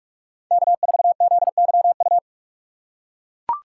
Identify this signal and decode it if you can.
Morse code, heard as a keyed tone, about 41 wpm, 700 Hz. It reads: K4ZYU
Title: USA Call Signs - 5 Characters - 3x Character Spacing - 45wpm